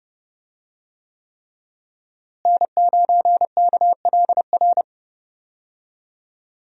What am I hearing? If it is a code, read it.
Morse code, heard as a keyed tone, about 30 wpm, 700 Hz. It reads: N9KLR
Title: USA Call Signs - 5 Characters - Repeated 3x - 30wpm